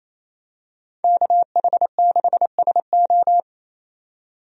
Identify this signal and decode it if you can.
Morse code, heard as a keyed tone, about 28 wpm, 700 Hz. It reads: KH6SO
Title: DXCC-100 Call Signs - 5 Characters - 28wpm